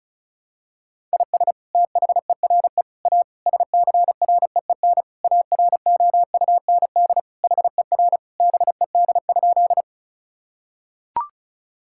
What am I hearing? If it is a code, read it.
Morse code, heard as a keyed tone, about 36 wpm, 700 Hz. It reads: IS THERE A SCREEN AROUND HER BED?
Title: Sentences from Top 1000 Words - Review Part 6 - 35wpm